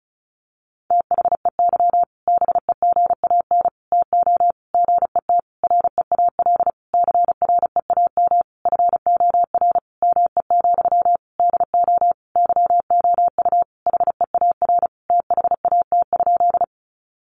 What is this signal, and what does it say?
Morse code, heard as a keyed tone, about 34 wpm, 700 Hz. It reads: THEY BEGAN TO GET REAL CREAM FOR ME, DO YOU HEAR THAT?
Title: Sentences from Top 1500 Words 35wpm